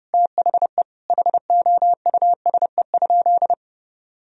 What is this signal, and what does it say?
Morse code, heard as a keyed tone, about 32 wpm, 700 Hz. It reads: THE HOUSE?
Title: Sentences from Top 1000 Words - Review Part 2 - 30wpm